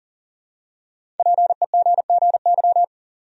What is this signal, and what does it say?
Morse code, heard as a keyed tone, about 40 wpm, 700 Hz. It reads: PEGGY
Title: USA Common Names - Max 5 Letters - 40wpm